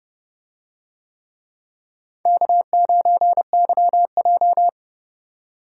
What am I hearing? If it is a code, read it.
Morse code, heard as a keyed tone, about 29 wpm, 700 Hz. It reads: K9YJ